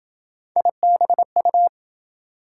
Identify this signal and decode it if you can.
Morse code, heard as a keyed tone, about 27 wpm, 700 Hz. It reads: IBU